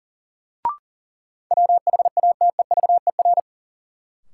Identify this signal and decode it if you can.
Morse code, heard as a keyed tone, about 40 wpm, 700 Hz. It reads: WHATEVER
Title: Sentences from Top 1100 Words - New Word Drill - 40wpm